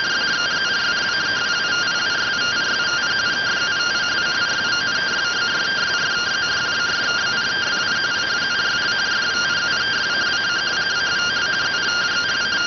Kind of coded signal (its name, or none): two-tone teleprinter signal